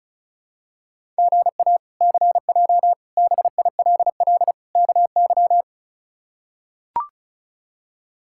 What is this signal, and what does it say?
Morse code, heard as a keyed tone, about 36 wpm, 700 Hz. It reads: GA CJ BILL KY
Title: K1USN SST QSO Practice - 35wpm